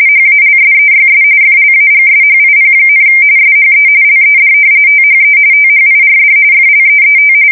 RTTY, about 45.45 baud. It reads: HALSEYS FLAGSHIP, SHE LED THE THIRD FLEET
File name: fsk170 / battleshipnj